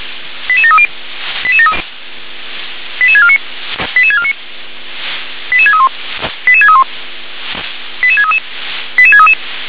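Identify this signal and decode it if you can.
Five-tone selective call (ZVEI), heard as quick runs of five tones, tones of about 70 ms. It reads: 88730 88649 88649 88631 88631 88633 88633
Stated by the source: ZVEI five-tone selective call received on an SDR.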